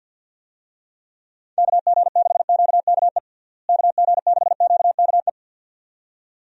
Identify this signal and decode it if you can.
Morse code, heard as a keyed tone, about 51 wpm, 700 Hz. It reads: KG6QKE KG6QKE
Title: USA Call Signs - 6 Characters - Repeated 2x - 50wpm